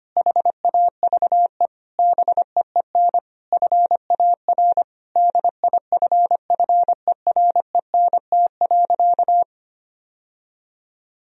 Morse code, heard as a keyed tone, about 25 wpm, 700 Hz. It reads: HAVE BEEN FAR DIFFERENT.
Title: Sentences from Top 1000 Words - Review Part 3 - 25wpm